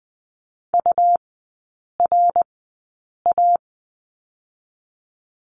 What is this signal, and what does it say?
Morse code, heard as a keyed tone, about 20 wpm, spaced as for 8 wpm, 700 Hz. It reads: URA
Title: 100 Lesson 17e - Letter u - Three Characters 20wpm